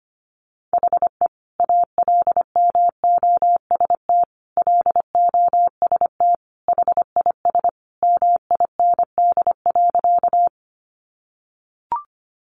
Morse code, heard as a keyed tone, about 25 wpm, 700 Hz. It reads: HE ALMOST LOST HIS MIND.
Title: Sentences from Top 1400 Words 25wpm